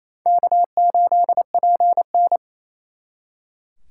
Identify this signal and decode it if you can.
Morse code, heard as a keyed tone, about 35 wpm, 700 Hz. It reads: K8PN
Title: Speed Racing - USA 1x2 Call Signs 28wpm